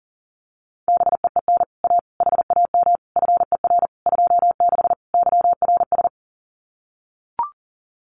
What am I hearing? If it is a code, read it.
Morse code, heard as a keyed tone, about 38 wpm, 700 Hz. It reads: BEEN A HAM FER 26 YRS
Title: Common QSO Phrases Encore 40wpm